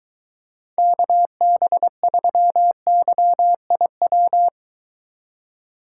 Morse code, heard as a keyed tone, about 23 wpm, 700 Hz. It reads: KB3YIW